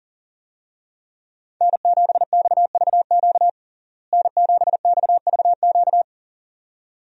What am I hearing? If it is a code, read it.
Morse code, heard as a keyed tone, about 40 wpm, 700 Hz. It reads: N7XVQ N7XVQ
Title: USA Call Signs - 5 Characters - Repeated 2x - 40wpm